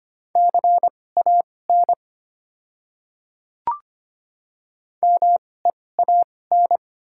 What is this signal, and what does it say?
Morse code, heard as a keyed tone, about 25 wpm, spaced as for 18 wpm, 690 Hz. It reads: CAN MEAN
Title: 2x Character Spacing - Top 200 Words - 25wpm